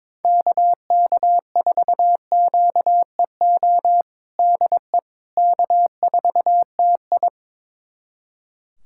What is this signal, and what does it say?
Morse code, heard as a keyed tone, about 21 wpm, 700 Hz. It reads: KK4QEO DE K4TI
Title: USA Callsign DE Callsign 22wpm